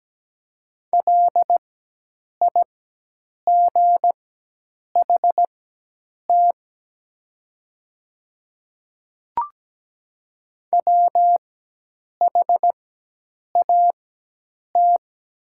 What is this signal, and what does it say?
Morse code, heard as a keyed tone, about 17 wpm, spaced as for 8 wpm, 700 Hz. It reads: LIGHT WHAT
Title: 4x Character Spacing - Top 300 Words - 17wpm